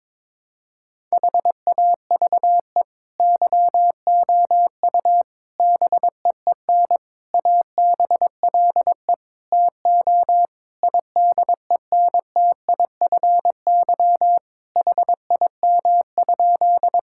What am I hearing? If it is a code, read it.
Morse code, heard as a keyed tone, about 22 wpm, 700 Hz. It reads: HAVE YOU BEEN ABLE TO IDENTIFY HIM?
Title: Sentences from Top 1000 Words - Review Part 6 - 22wpm